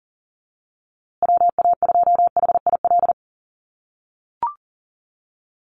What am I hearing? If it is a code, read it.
Morse code, heard as a keyed tone, about 40 wpm, 700 Hz. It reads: WA2HIL
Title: USA Call Signs Repeated 3x 40wpm